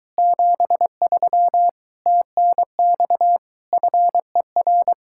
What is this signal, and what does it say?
Morse code, heard as a keyed tone, about 23 wpm, 700 Hz. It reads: 73 TNX FER